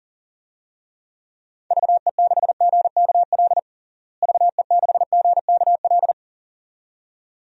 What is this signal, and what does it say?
Morse code, heard as a keyed tone, about 40 wpm, 700 Hz. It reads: VE6GKL VE6GKL
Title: DXCC-100 Call Signs - 6 Characters - Repeated 2x - 40wpm